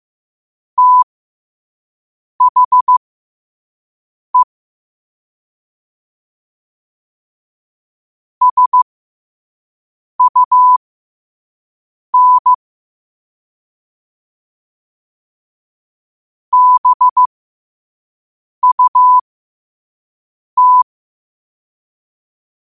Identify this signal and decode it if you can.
Morse code, heard as a keyed tone, about 15 wpm, spaced as for 5 wpm, 1000 Hz. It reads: THE SUN BUT
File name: qotdfiles / 5wpm